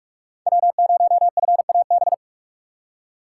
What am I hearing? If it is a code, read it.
Morse code, heard as a keyed tone, about 44 wpm, 690 Hz. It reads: W0FAB
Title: USA Call Signs - 5 Characters - 6x Character Spacing - 45wpm